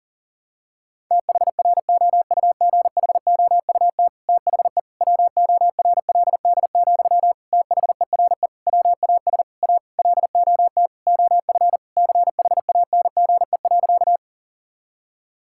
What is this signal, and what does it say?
Morse code, heard as a keyed tone, about 37 wpm, 700 Hz. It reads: THROUGHOUT THE WORLD, THERE WAS A LOT OF CHANGE.
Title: Sentences from Top 1000 Words - Review Part 5 - 40wpm